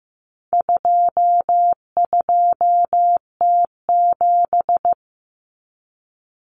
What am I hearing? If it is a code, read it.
Morse code, heard as a keyed tone, about 15 wpm, 700 Hz. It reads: 22T7
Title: Common Cut Numbers - 4-Digit 15wpm